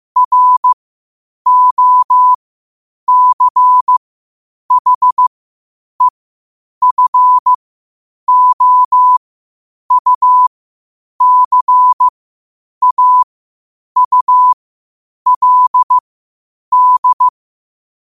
Morse code, heard as a keyed tone, about 15 wpm, spaced as for 8 wpm, 1000 Hz. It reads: ROCHEFOUCAULD
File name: qotdfiles / 8wpm